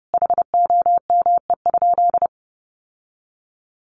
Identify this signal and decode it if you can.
Morse code, heard as a keyed tone, about 30 wpm, 700 Hz. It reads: HOME?
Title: Sentences from Top 800 Words 30wpm